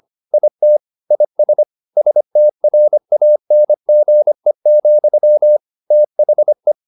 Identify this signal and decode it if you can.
Morse code, heard as a keyed tone, about 25 wpm, 600 Hz. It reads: IT IS STRANGE, THE
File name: the old man and the sea part 1 25wpm